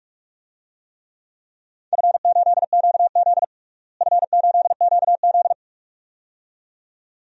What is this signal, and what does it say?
Morse code, heard as a keyed tone, about 42 wpm, 700 Hz. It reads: F8QZ F8QZ
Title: DXCC-100 Call Signs - 4 Characters - Repeated 2x - 45wpm